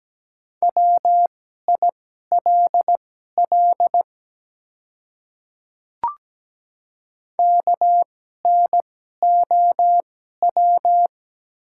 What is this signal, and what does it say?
Morse code, heard as a keyed tone, about 17 wpm, spaced as for 12 wpm, 690 Hz. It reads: WILL KNOW
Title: 2x Character Spacing - Top 100 Words - 17wpm